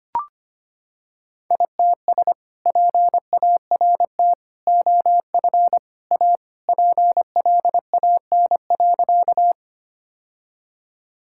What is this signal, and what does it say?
Morse code, heard as a keyed tone, about 25 wpm, 700 Hz. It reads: ITS PART OF A PLAN.
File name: practice / 25wpm / Sentences with 3-Word Phrases - 25wpm